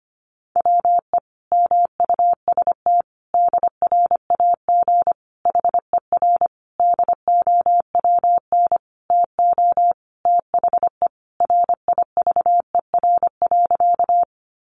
Morse code, heard as a keyed tone, about 25 wpm, 700 Hz. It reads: WE MUST DRAG HER DOWN TO THE RIVER.
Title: Sentences from Top 1500 Words 25wpm